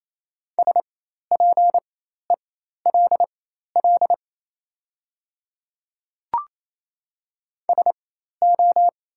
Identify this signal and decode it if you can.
Morse code, heard as a keyed tone, about 27 wpm, spaced as for 13 wpm, 700 Hz. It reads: SPELL SO
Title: 4x Character Spacing - Top 200 Words - 28wpm